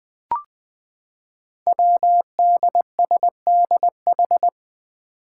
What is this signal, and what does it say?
Morse code, heard as a keyed tone, about 20 wpm, 700 Hz. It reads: WDSDH
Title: Five Challenging Letter-Number Combinations 20wpm